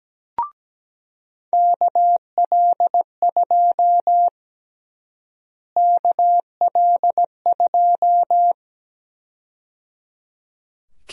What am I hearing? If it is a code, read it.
Morse code, heard as a keyed tone, about 17 wpm, 700 Hz. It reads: KL2 KL2
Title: USA Call Sign Prefixes - Repeated 2x - 17wpm